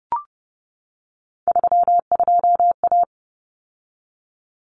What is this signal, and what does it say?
Morse code, heard as a keyed tone, about 30 wpm, 700 Hz. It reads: 32A
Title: Common Cut Numbers - 3-Digit 30wpm